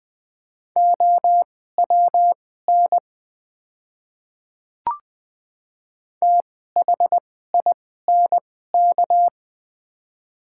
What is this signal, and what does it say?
Morse code, heard as a keyed tone, about 21 wpm, spaced as for 15 wpm, 700 Hz. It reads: OWN THINK
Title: 2x Character Spacing - Top 300 Words - 20wpm